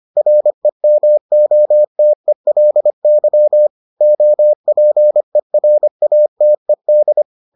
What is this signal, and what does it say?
Morse code, heard as a keyed tone, about 25 wpm, 600 Hz. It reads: REMOTELY OPERATED